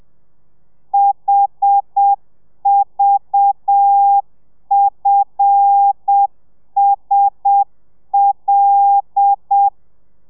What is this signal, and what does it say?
Morse code, heard as a keyed tone, about 7 wpm, 800 Hz. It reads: HVFSL
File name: telegrafia / i2tfj / Lez5 30